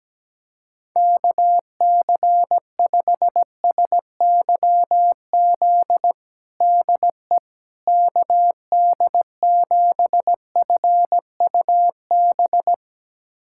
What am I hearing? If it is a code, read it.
Morse code, heard as a keyed tone, about 17 wpm, 700 Hz. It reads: KC5SYZ DE KD7FUB
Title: USA Callsign DE Callsign - 6 Characters - 17wpm